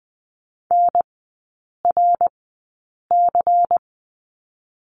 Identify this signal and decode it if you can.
Morse code, heard as a keyed tone, about 20 wpm, spaced as for 8 wpm, 690 Hz. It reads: NRC
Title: 093 Lesson 16e - Letter c - Three Characters 20wpm